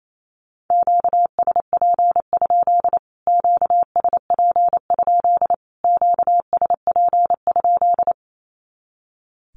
Morse code, heard as a keyed tone, about 28 wpm, 700 Hz. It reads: QSP? QSP? QSP?
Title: Q-Codes Repeated 3x 28wpm